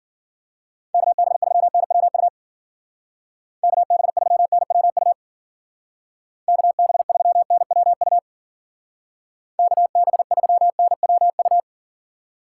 Morse code, heard as a keyed tone, about 49 wpm, 700 Hz. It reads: KB3NWU KB3NWU KB3NWU KB3NWU
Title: Speed Racing - USA 2x3 Call Signs 40wpm